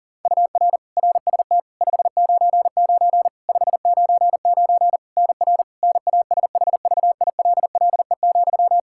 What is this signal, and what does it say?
Morse code, heard as a keyed tone, about 40 wpm, 700 Hz. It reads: UR RST 599 599 NR NASHVILLE,
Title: Practice QSO - Exchange 1 - Beginning - 40wpm